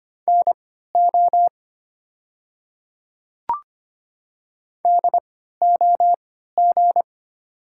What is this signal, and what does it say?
Morse code, heard as a keyed tone, about 25 wpm, spaced as for 14 wpm, 700 Hz. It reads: NO DOG
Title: 3x Character Spacing - Top 500 Words - 25wpm